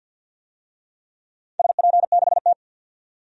Morse code, heard as a keyed tone, about 52 wpm, 700 Hz. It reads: SP6T
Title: DXCC-100 Call Signs - 4 Characters - 50wpm